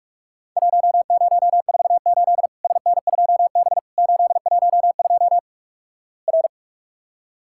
Morse code, heard as a keyed tone, about 46 wpm, 700 Hz. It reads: 1048 SN2B 812 R
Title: Worked All Europe DX Contest - QTC Exchange - 45wpm